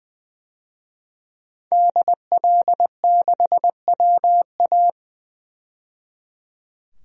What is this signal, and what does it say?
Morse code, heard as a keyed tone, about 21 wpm, 700 Hz. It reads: DL6WA
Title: DXCC-100 Call Signs - 5 Characters - 20wpm